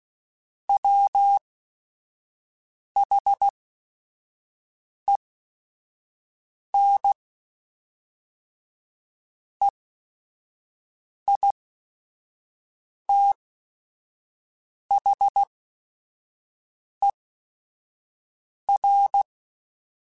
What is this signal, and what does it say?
Morse code, heard as a keyed tone, about 16 wpm, spaced as for 5 wpm, 800 Hz. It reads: WHEN EITHER
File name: Technical Subjects / Technical Subjects-mp3s / AND Gate @5